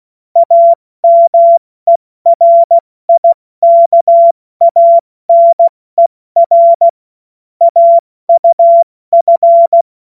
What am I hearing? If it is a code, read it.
Morse code, heard as a keyed tone, about 16 wpm, 680 Hz. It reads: AMERIKANER AUF
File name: Witz-05-August-2025-70-F80